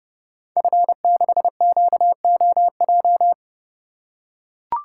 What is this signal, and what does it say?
Morse code, heard as a keyed tone, about 30 wpm, 700 Hz. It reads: F6QOJ
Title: Code Groups - 5 Letter:Numbers - 30wpm